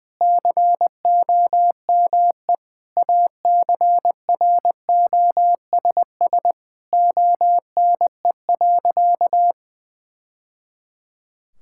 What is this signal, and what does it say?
Morse code, heard as a keyed tone, about 20 wpm, 700 Hz. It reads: COME ACROSS ONE.
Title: Sentences from Top 1000 Words - Review Part 5 - 20wpm